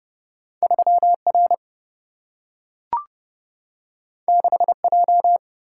Morse code, heard as a keyed tone, about 30 wpm, 700 Hz. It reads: 3 R 6 J
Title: Single-Letter Single-Number 30wpm